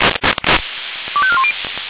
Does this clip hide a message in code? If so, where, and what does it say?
Yes: five-tone selective call (ZVEI) from 1.2 to 1.5 s, heard as quick runs of five tones, tones of about 70 ms. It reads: 25410